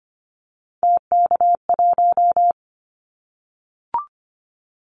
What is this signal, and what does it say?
Morse code, heard as a keyed tone, about 25 wpm, 700 Hz. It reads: TK1
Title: Callsign Prefixes for Top 100 DXCC - 25wpm